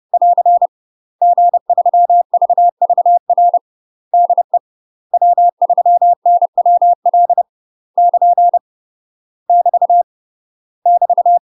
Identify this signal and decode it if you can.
Morse code, heard as a keyed tone, about 30 wpm, 700 Hz. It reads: <AR> G3VVR DE W3NWL <KN> <BT> <BT>
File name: UK MASTER QSOs TRAINING / 30WPM / UK MASTER QSOs 0044